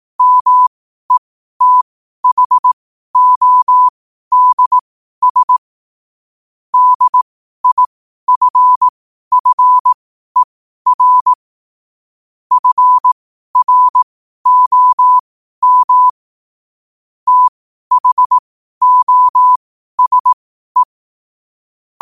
Morse code, heard as a keyed tone, about 18 wpm, spaced as for 12 wpm, 1000 Hz. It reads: METHODS DIFFER FROM THOSE
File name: qotdfiles / 12wpm